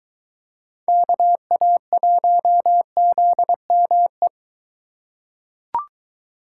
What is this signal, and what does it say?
Morse code, heard as a keyed tone, about 23 wpm, 700 Hz. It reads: KA1ZME